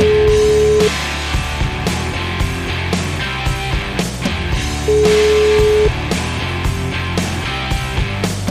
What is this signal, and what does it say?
Ringback tone (Europe, 425 Hz), with noise.